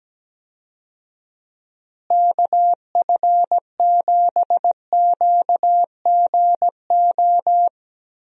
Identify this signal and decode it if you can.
Morse code, heard as a keyed tone, about 17 wpm, 700 Hz. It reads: KF7QGO
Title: USA Call Signs - 6 Characters - Repeated 2x - 17wpm